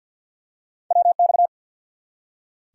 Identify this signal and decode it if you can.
Morse code, heard as a keyed tone, about 48 wpm, 700 Hz. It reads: WX
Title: Top 90 QSO Elements 50wpm